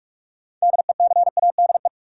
Morse code, heard as a keyed tone, about 45 wpm, 700 Hz. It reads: DECADE
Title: Sets of 5 Words - New Word Drill - 45wpm